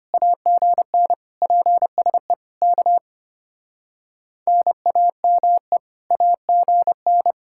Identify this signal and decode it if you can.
Morse code, heard as a keyed tone, about 30 wpm, 700 Hz. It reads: AGN PSE K NAME AGN
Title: Speed-Racing Common QSO Phrases - 25wpm